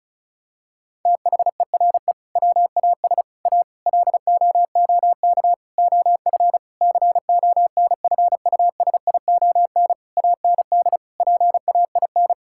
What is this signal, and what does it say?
Morse code, heard as a keyed tone, about 34 wpm, 700 Hz. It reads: THERE WAS A LOOK OF CONFUSION AND PAIN
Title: Sentences from Top 1600 Words 35wpm